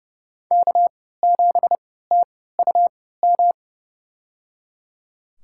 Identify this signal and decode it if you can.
Morse code, heard as a keyed tone, about 30 wpm, spaced as for 17 wpm, 700 Hz. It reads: K7TUM
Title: USA Call Signs - 5 Characters - 3x Character Spacing - 30wpm